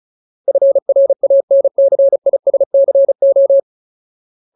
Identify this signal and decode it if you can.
Morse code, heard as a keyed tone, about 35 wpm, 550 Hz. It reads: FRANCISCO
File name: Ultimate Word List / 35WPM Ultimate